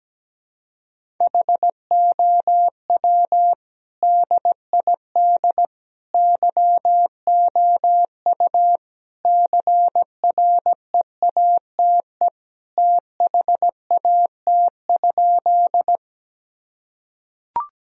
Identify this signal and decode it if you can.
Morse code, heard as a keyed tone, about 17 wpm, 700 Hz. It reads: HOW DID YOU CREATE THAT?
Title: Sentences from Top 1000 Words - Review Part 3 - 17wpm